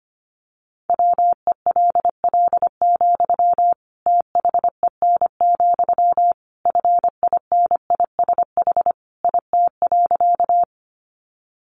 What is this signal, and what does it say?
Morse code, heard as a keyed tone, about 25 wpm, 700 Hz. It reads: WELL, THEN, FINISH IT.
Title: Sentences from Top 1200 Words 25wpm